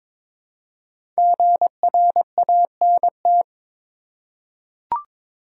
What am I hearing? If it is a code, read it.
Morse code, heard as a keyed tone, about 22 wpm, 700 Hz. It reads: GRANT
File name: 5 Letter Words - First 3 Letters Given - 22wpm